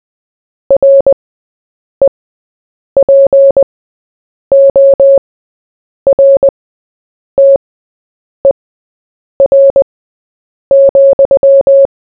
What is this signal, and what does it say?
Morse code, heard as a keyed tone, about 20 wpm, spaced as for 8 wpm, 550 Hz. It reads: REPORTER,
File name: Short-Story-4-8wpm